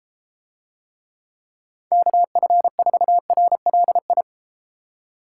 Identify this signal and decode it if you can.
Morse code, heard as a keyed tone, about 33 wpm, 700 Hz. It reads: KF4RLI